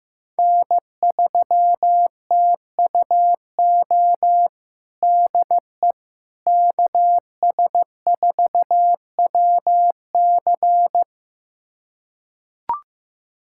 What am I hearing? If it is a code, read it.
Morse code, heard as a keyed tone, about 15 wpm, 700 Hz. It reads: N3TUO DE KS4WC
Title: USA Callsign DE Callsign - 5 Characters - 15wpm